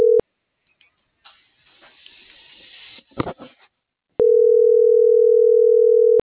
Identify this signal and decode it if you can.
Ringback tone (North America), 440 + 480 Hz.